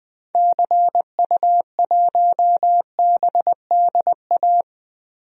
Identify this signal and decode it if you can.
Morse code, heard as a keyed tone, about 21 wpm, 700 Hz. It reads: CU1BDA
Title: DXCC-100 Call Signs - 6 Characters - Repeated 2x - 20wpm